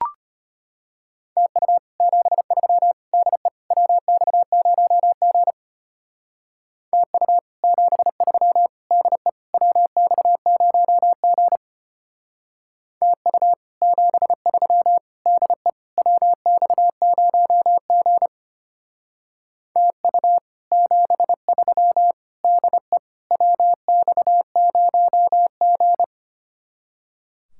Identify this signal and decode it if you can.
Morse code, heard as a keyed tone, about 30 wpm, 700 Hz. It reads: TU 73 DE WX0G TU 73 DE WX0G TU 73 DE WX0G TU 73 DE WX0G
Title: Speed-Racing Common QSO Phrases - 25wpm